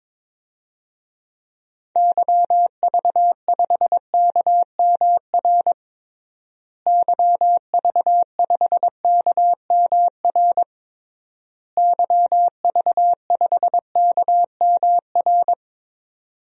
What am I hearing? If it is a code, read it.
Morse code, heard as a keyed tone, about 21 wpm, 700 Hz. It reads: YV5KMR YV5KMR YV5KMR
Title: DXCC-100 Call Signs - 6 Characters - Repeated 3x - 22wpm